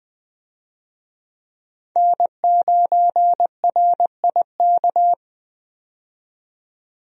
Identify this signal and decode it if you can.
Morse code, heard as a keyed tone, about 20 wpm, 700 Hz. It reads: N9RIK
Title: USA Call Signs - 5 Characters - 20wpm